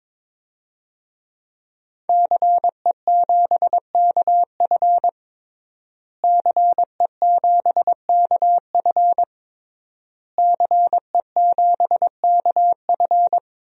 Morse code, heard as a keyed tone, about 22 wpm, 700 Hz. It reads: CE7KF CE7KF CE7KF
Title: DXCC-100 Call Signs - 5 Characters - Repeated 3x - 22wpm